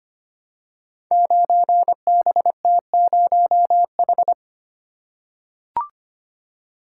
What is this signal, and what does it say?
Morse code, heard as a keyed tone, about 25 wpm, 700 Hz. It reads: 9BT0H
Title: Code Groups - 5 Letter:Numbers - 25wpm